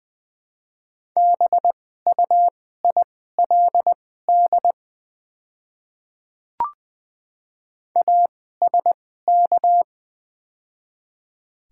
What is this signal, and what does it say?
Morse code, heard as a keyed tone, about 21 wpm, spaced as for 15 wpm, 700 Hz. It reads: BUILD ASK
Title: 2x Character Spacing - Top 200 Words - 20wpm